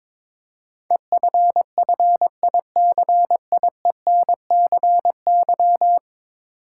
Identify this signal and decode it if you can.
Morse code, heard as a keyed tone, about 23 wpm, 700 Hz. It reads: EFFICIENCY
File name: practice / 22wpm / 10 Letter Words - First 3 Letters Given - 22wpm